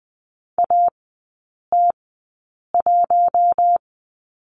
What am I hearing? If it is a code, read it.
Morse code, heard as a keyed tone, about 20 wpm, spaced as for 8 wpm, 700 Hz. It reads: AT1
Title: 060 Lesson 11e - Letter h - Three Characters 20wpm